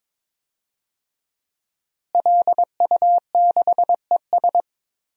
Morse code, heard as a keyed tone, about 22 wpm, 700 Hz. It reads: LU6ES
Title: DXCC-100 Call Signs - 5 Characters - Repeated 2x - 22wpm